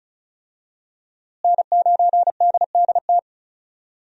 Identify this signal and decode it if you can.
Morse code, heard as a keyed tone, about 36 wpm, 700 Hz. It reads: N9DDT
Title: USA Call Signs - 5 Characters - Repeated 3x - 35wpm